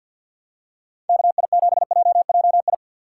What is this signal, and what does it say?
Morse code, heard as a keyed tone, about 47 wpm, 700 Hz. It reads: KI7JJI
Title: USA Call Signs - 6 Characters - 4x Character Spacing - 50wpm